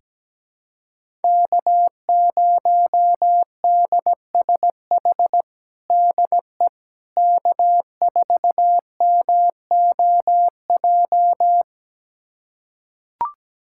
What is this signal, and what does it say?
Morse code, heard as a keyed tone, about 17 wpm, 700 Hz. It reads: K0DSH DE K4MOJ
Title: USA Callsign DE Callsign - 5 Characters - 17wpm